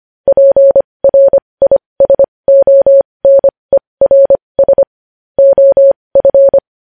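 Morse code, heard as a keyed tone, about 25 wpm, spaced as for 20 wpm, 550 Hz. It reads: PRISONERS OF
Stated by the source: Morse practice sending news headlines at 20 wpm.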